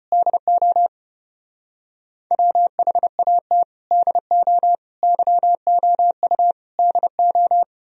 Morse code, heard as a keyed tone, about 32 wpm, 700 Hz. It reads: DO WHAT DO YOU DO